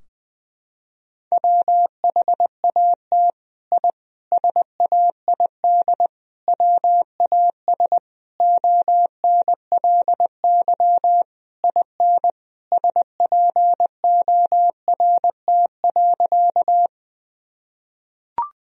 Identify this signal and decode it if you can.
Morse code, heard as a keyed tone, about 21 wpm, 700 Hz. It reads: WHAT I SAID WAS ONLY IN SPORT.
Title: Sentences from Top 1000 Words - Review Part 4 - 20wpm